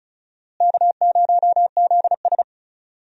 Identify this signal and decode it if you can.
Morse code, heard as a keyed tone, about 36 wpm, 700 Hz. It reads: K0ZS
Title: USA Call Signs - 4 Characters - Repeated 2x - 35wpm